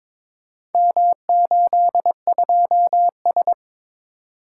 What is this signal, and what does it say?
Morse code, heard as a keyed tone, about 22 wpm, 700 Hz. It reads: M82S
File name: practice / 22wpm / Four Letter-Number Combinations - 22wpm